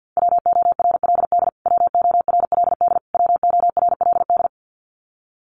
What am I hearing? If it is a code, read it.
Morse code, heard as a keyed tone, about 50 wpm, 700 Hz. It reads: WORLD WORLD WORLD
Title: Top 200 Words Repeated 50wpm